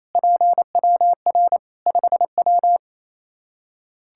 Morse code, heard as a keyed tone, about 27 wpm, 700 Hz. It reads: PWR 5W
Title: Practice QSO - Exchange 2 - Info - 28wpm